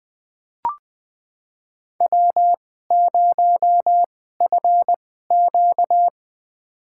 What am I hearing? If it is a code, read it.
Morse code, heard as a keyed tone, about 21 wpm, spaced as for 15 wpm, 700 Hz. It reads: W0FQ
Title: USA Call Signs - 4 Characters - 2x Character Spacing - 20wpm